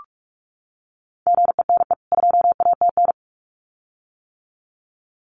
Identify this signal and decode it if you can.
Morse code, heard as a keyed tone, about 44 wpm, 700 Hz. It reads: GENE 2ATN
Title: CWT Exchange 45wpm